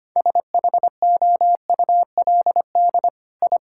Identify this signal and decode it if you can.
Morse code, heard as a keyed tone, about 25 wpm, 700 Hz. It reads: SHOULD I
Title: Sentences from Top 1000 Words - Review Part 6 - 25wpm